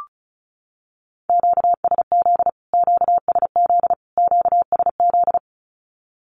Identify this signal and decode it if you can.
Morse code, heard as a keyed tone, about 34 wpm, 700 Hz. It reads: QSZ QSZ QSZ
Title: Q-Codes Repeated 3x 35wpm